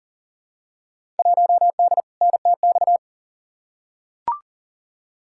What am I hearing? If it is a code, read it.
Morse code, heard as a keyed tone, about 40 wpm, 700 Hz. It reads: 1D NTX
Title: Field Day Exchanges - 40wpm